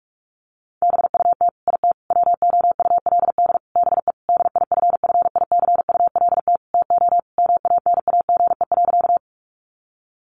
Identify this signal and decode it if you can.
Morse code, heard as a keyed tone, about 44 wpm, 700 Hz. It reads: BUT IT WOULD BE DIFFICULT TO MANAGE.
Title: Sentences from Top 1400 Words 45wpm